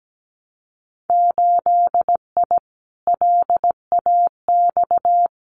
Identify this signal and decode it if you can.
Morse code, heard as a keyed tone, about 17 wpm, 700 Hz. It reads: 8I LAX
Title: Winter Field Day Exchanges Encore 17wpm